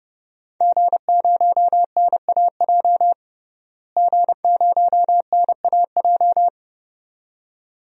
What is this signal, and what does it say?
Morse code, heard as a keyed tone, about 32 wpm, 700 Hz. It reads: G0NAJ G0NAJ
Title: DXCC-100 Call Signs - 5 Characters - Repeated 2x - 30wpm